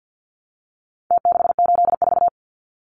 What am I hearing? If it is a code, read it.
Morse code, heard as a keyed tone, about 49 wpm, 700 Hz. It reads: T684
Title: Numbers - Contest Style - 4-Digit 50wpm